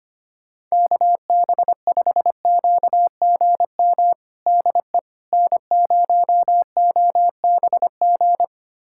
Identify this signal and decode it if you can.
Morse code, heard as a keyed tone, about 25 wpm, 700 Hz. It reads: KB5QGM DE N0OBG
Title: USA Callsign DE Callsign 25wpm